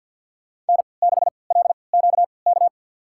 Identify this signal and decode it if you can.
Morse code, heard as a keyed tone, about 51 wpm, spaced as for 29 wpm, 700 Hz. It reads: N6LQK
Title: USA Call Signs - 5 Characters - 3x Character Spacing - 50wpm